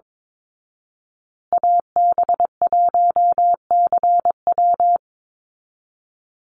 Morse code, heard as a keyed tone, about 22 wpm, 700 Hz. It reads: AB1CW
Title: Call Signs Repeated 2x - 22wpm